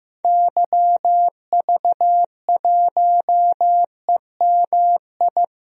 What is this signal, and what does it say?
Morse code, heard as a keyed tone, about 15 wpm, 700 Hz. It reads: YV1EMI
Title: DXCC-100 Call Signs - 6 Characters - 15wpm